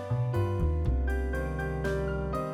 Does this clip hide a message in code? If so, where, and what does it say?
No.